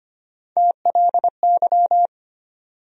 Morse code, heard as a keyed tone, about 25 wpm, 700 Hz. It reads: TLY
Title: USA Call Sign Postfixes - 3 Letters 25wpm